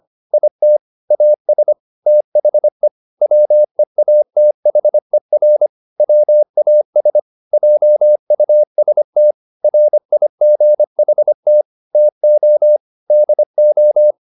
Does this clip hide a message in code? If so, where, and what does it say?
Yes: Morse code from 0.3 to 14.1 s, heard as a keyed tone, about 25 wpm, 600 Hz. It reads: IT AS THE WEATHER WAS JUST RIGHT TO DO